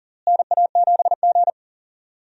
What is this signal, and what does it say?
Morse code, heard as a keyed tone, about 40 wpm, 700 Hz. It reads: NA7G
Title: USA Call Signs - 4 Characters - Repeated 3x - 40wpm